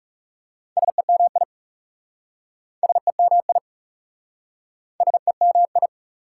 Morse code, heard as a keyed tone, about 37 wpm, 700 Hz. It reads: SEMI SEMI SEMI
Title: Speed-Racing Prefixes and Suffixes 30wpm